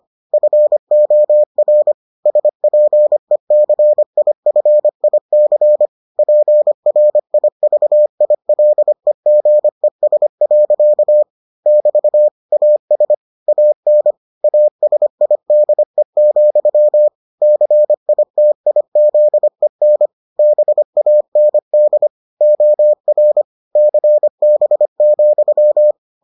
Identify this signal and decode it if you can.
Morse code, heard as a keyed tone, about 25 wpm, 600 Hz. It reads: FOR SPECIFIC PRIVILEGES. <BT> AS AN ASIDE, CITIZEN BAND OR CB,